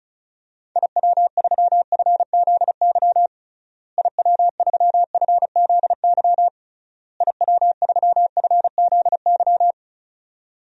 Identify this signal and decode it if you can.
Morse code, heard as a keyed tone, about 34 wpm, 700 Hz. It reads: IW3FZY IW3FZY IW3FZY
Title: DXCC-100 Call Signs - 6 Characters - Repeated 3x - 35wpm